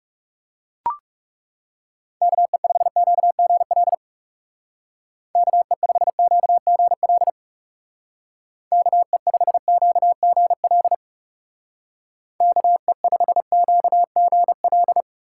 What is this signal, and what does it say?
Morse code, heard as a keyed tone, about 35 wpm, 700 Hz. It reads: KE5QGL KE5QGL KE5QGL KE5QGL
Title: Speed Racing - USA 2x3 Call Signs 30wpm